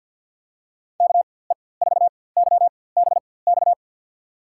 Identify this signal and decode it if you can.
Morse code, heard as a keyed tone, about 47 wpm, spaced as for 23 wpm, 700 Hz. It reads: KE4YBX
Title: USA Call Signs - 6 Characters - 4x Character Spacing - 50wpm